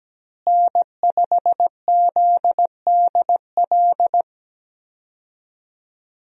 Morse code, heard as a keyed tone, about 17 wpm, 700 Hz. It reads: N5ZDL
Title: USA Call Signs - 5 Characters - Repeated 3x - 17wpm